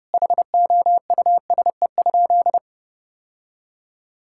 Morse code, heard as a keyed tone, about 30 wpm, 700 Hz. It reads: HOUSE?
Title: Sentences from Top 1000 Words - Review Part 5 - 30wpm